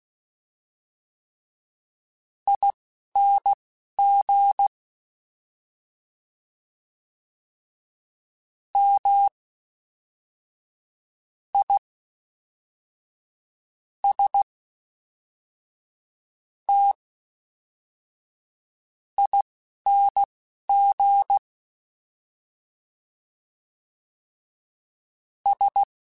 Morse code, heard as a keyed tone, about 16 wpm, spaced as for 5 wpm, 800 Hz. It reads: ING MISTINGS